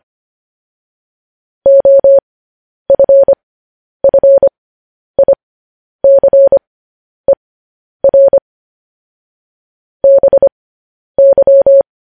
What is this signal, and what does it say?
Morse code, heard as a keyed tone, about 25 wpm, spaced as for 10 wpm, 550 Hz. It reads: OFFICER BY
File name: podcast / News headlines-1x-25x10